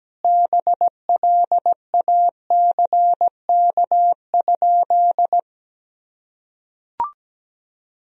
Morse code, heard as a keyed tone, about 17 wpm, 700 Hz. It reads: BLACK?
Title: Sentences from Top 400 Words and Abbreviations 17wpm